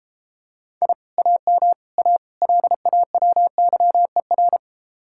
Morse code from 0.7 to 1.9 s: I AM

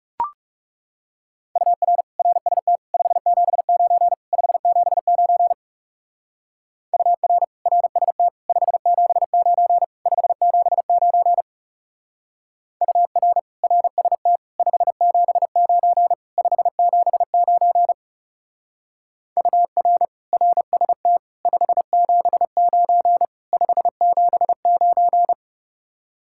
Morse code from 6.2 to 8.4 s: UR RST

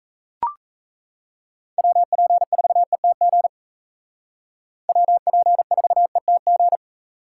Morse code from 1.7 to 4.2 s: WP4ETG

